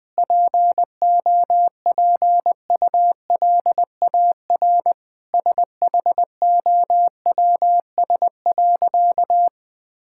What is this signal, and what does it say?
Morse code, heard as a keyed tone, about 21 wpm, 700 Hz. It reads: POPULAR SHOWS.
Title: Sentences with 5-Word Phrases - Spoken Beginning - 20wpm